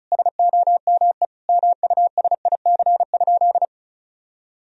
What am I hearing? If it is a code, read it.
Morse code, heard as a keyed tone, about 34 wpm, 700 Hz. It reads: SOME MUSIC?
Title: Sentences from Top 1000 Words - Review Part 1 - 35wpm